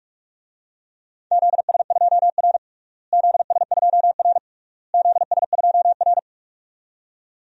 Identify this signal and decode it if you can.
Morse code, heard as a keyed tone, about 43 wpm, 690 Hz. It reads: ZS2R ZS2R ZS2R
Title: DXCC-100 Call Signs - 4 Characters - Repeated 3x - 45wpm